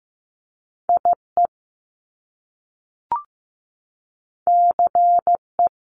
Morse code, heard as a keyed tone, about 15 wpm, 700 Hz. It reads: IE CE
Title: Top Two Letter Combinations in Words 15wpm